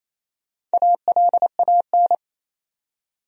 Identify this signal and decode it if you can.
Morse code, heard as a keyed tone, about 27 wpm, 700 Hz. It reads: ALAN
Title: USA Common Names - Max 4 Letters - 28wpm